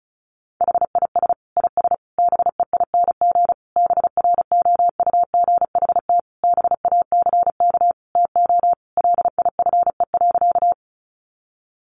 Morse code, heard as a keyed tone, about 34 wpm, 700 Hz. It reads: HIS IS BEING BROUGHT BACK TO LIFE.
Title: Sentences from Top 600 Words 35wpm